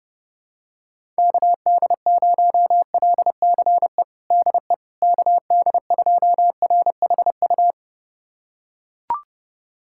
Morse code, heard as a keyed tone, about 30 wpm, 700 Hz. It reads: KD0LCE DE KD2RHU
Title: USA Callsign DE Callsign - 6 Characters - 30wpm